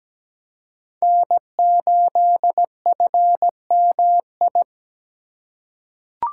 Morse code, heard as a keyed tone, about 17 wpm, 700 Hz. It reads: N8FMI
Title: USA Call Signs - 5 Characters - 3x Character Spacing - 17wpm